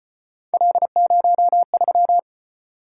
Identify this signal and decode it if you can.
Morse code, heard as a keyed tone, about 32 wpm, 700 Hz. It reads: L03